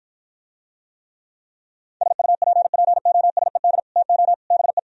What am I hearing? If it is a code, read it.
Morse code, heard as a keyed tone, about 52 wpm, 700 Hz. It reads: SUPPOSED TO BE